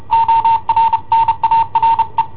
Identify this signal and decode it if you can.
Morse code, heard as a keyed tone, about 31 wpm, 910 Hz. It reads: ORNARE
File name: i7ohp / cassetta i7ohp B